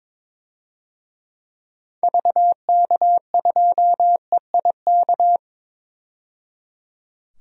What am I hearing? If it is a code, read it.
Morse code, heard as a keyed tone, about 22 wpm, 700 Hz. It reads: VK2EIK
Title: DXCC-100 Call Signs - 6 Characters - 22wpm